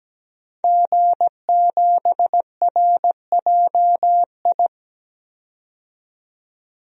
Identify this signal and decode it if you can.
Morse code, heard as a keyed tone, about 17 wpm, 700 Hz. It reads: G7RJI
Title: DXCC-100 Call Signs - 5 Characters - 17wpm